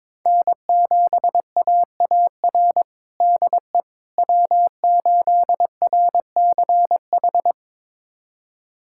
Morse code, heard as a keyed tone, about 22 wpm, 700 Hz. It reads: N7AAR DE W8RCH
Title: USA Callsign DE Callsign - 5 Characters - 22wpm